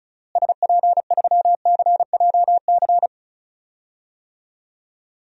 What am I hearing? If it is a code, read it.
Morse code, heard as a keyed tone, about 36 wpm, 700 Hz. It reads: SP3CJC
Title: DXCC-100 Call Signs - 6 Characters - Repeated 3x - 35wpm